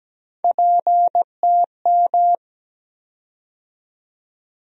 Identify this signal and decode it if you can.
Morse code, heard as a keyed tone, about 17 wpm, 700 Hz. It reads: PTM
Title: USA Call Sign Postfixes - 3 Letters 17wpm